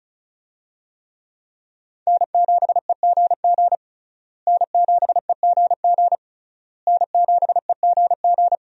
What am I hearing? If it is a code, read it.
Morse code, heard as a keyed tone, about 36 wpm, 700 Hz. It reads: N7EGG N7EGG N7EGG
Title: USA Call Signs - 5 Characters - Repeated 3x - 35wpm